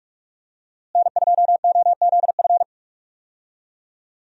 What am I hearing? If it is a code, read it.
Morse code, heard as a keyed tone, about 41 wpm, 700 Hz. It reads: N2OZF
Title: USA Call Signs - 5 Characters - Repeated 2x - 45wpm